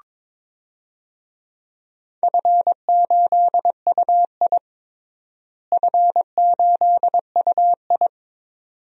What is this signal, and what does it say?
Morse code, heard as a keyed tone, about 22 wpm, 700 Hz. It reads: F8UI F8UI
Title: DXCC-100 Call Signs - 4 Characters - Repeated 2x - 22wpm